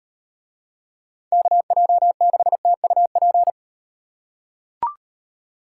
Morse code, heard as a keyed tone, about 38 wpm, 700 Hz. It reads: KJ6TUP